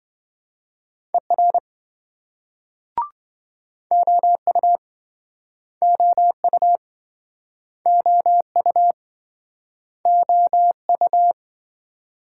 Morse code, heard as a keyed tone, about 29 wpm, 700 Hz. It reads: ER OU OU OU OU